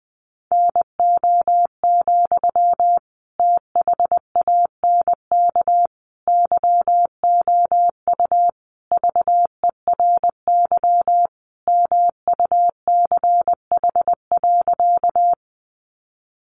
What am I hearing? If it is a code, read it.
Morse code, heard as a keyed tone, about 20 wpm, 700 Hz. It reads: NO, THANK YOU VERY MUCH.
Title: Sentences from Top 600 Words 20wpm